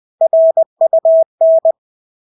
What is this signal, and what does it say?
Morse code, heard as a keyed tone, about 20 wpm, 650 Hz. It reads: RUN
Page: Engelske ord 20 wpm